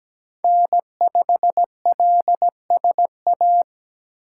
Morse code, heard as a keyed tone, about 17 wpm, 700 Hz. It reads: N5LSA
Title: USA Call Signs - 5 Characters - 8x Character Spacing - 17wpm